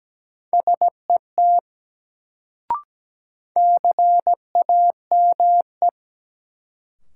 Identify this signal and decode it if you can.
Morse code, heard as a keyed tone, about 17 wpm, 700 Hz. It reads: SET CAME
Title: Sentences from Top 200 Words - New Word Drill - 17wpm